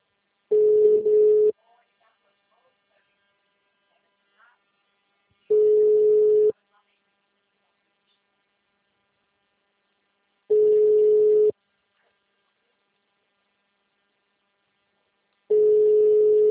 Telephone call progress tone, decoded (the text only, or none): ringback tone